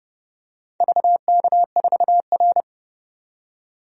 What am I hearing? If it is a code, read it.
Morse code, heard as a keyed tone, about 32 wpm, 700 Hz. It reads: VK4R
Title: DXCC-100 Call Signs - 4 Characters - 30wpm